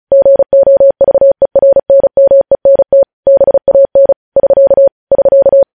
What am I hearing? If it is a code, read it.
Morse code, heard as a keyed tone, about 36 wpm, 550 Hz. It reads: GOVERNMENT BAN <SK> <SK>
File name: podcast / News headlines-1x-35x35